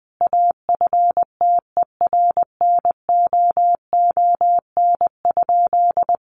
Morse code, heard as a keyed tone, about 20 wpm, 700 Hz. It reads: AFTERNOON?
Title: Sentences from Top 1500 Words 20wpm